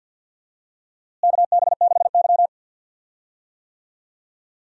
Morse code, heard as a keyed tone, about 52 wpm, 700 Hz. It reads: KB6Y
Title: USA Call Signs - 4 Characters - Repeated 3x - 50wpm